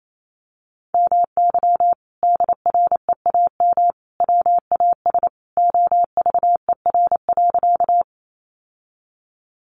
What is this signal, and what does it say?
Morse code, heard as a keyed tone, about 28 wpm, 700 Hz. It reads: MY DREAM WAS OVER.
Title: Sentences from Top 1500 Words 28wpm